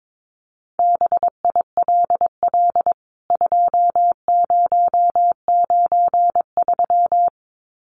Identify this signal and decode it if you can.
Morse code, heard as a keyed tone, about 22 wpm, 700 Hz. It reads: BILL 2093
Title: CWT Exchange 22wpm